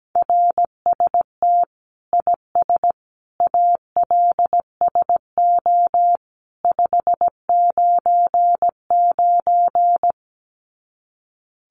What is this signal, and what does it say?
Morse code, heard as a keyed tone, about 17 wpm, 700 Hz. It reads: RST IS ALSO 599
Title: Common QSO Phrases 17wpm